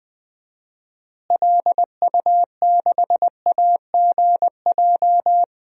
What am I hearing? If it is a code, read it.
Morse code, heard as a keyed tone, about 20 wpm, 700 Hz. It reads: LU6AGJ
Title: DXCC-100 Call Signs - 6 Characters - Repeated 3x - 20wpm